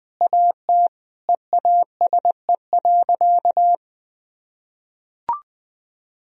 Morse code, heard as a keyed tone, about 20 wpm, 700 Hz. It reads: AT EASE.
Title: Sentences from Top 1000 Words - Review Part 1 - 20wpm